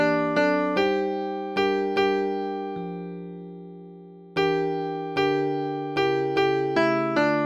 There is no code.